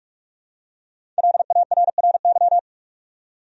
Morse code, heard as a keyed tone, about 48 wpm, 700 Hz. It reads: LARRY
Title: USA Common Names - Max 5 Letters - 45wpm